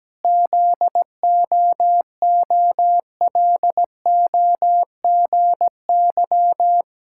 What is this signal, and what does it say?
Morse code, heard as a keyed tone, about 17 wpm, 700 Hz. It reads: ZOOLOGY
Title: Pangram Challenge v2 - 17wpm